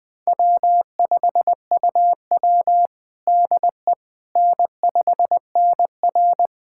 Morse code, heard as a keyed tone, about 20 wpm, 700 Hz. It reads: W5UW DE N5NR
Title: USA Callsign DE Callsign - 4 Characters - 20wpm